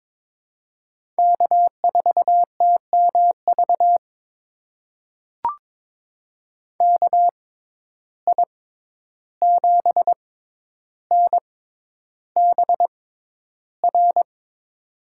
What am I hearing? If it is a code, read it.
Morse code, heard as a keyed tone, about 22 wpm, spaced as for 11 wpm, 700 Hz. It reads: K4TMV KI7NBR
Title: USA Call Signs - 6x Character Spacing - 22wpm